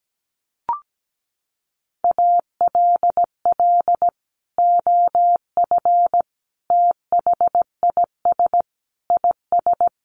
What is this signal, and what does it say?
Morse code, heard as a keyed tone, about 17 wpm, 700 Hz. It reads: ALL OF THIS IS
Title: Sentences from Top 1100 Words 17wpm